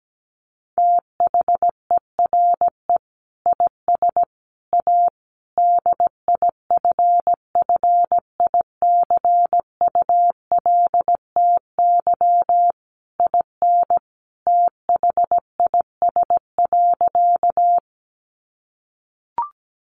Morse code, heard as a keyed tone, about 17 wpm, 700 Hz. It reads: THERE IS A DIFFICULTY IN THIS.
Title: Sentences from Top 1500 Words 17wpm